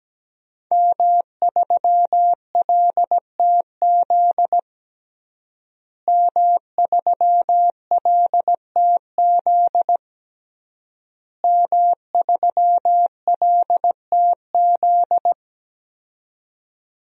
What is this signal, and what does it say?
Morse code, heard as a keyed tone, about 17 wpm, 700 Hz. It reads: M3LTZ M3LTZ M3LTZ
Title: DXCC-100 Call Signs - 5 Characters - Repeated 3x - 17wpm